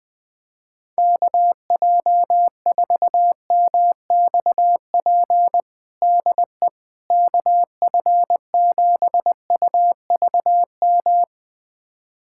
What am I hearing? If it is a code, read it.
Morse code, heard as a keyed tone, about 20 wpm, 700 Hz. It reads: KJ4MXP DE KF7UVM
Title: USA Callsign DE Callsign 20wpm